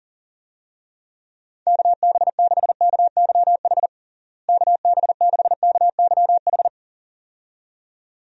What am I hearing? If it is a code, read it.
Morse code, heard as a keyed tone, about 40 wpm, 700 Hz. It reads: KB6KYH KB6KYH
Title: USA Call Signs - 6 Characters - Repeated 2x - 40wpm